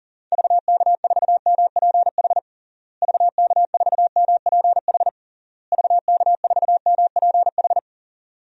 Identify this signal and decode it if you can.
Morse code, heard as a keyed tone, about 37 wpm, 700 Hz. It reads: VK4MPH VK4MPH VK4MPH
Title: DXCC-100 Call Signs - 6 Characters - Repeated 3x - 40wpm